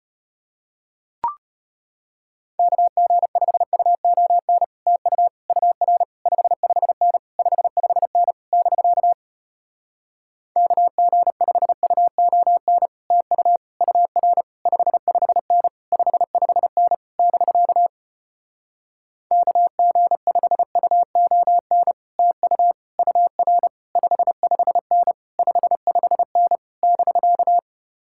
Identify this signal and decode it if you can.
Morse code, heard as a keyed tone, about 33 wpm, 700 Hz. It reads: KG5UON TU UR 55N 55N <BK> KG5UON TU UR 55N 55N <BK> KG5UON TU UR 55N 55N <BK>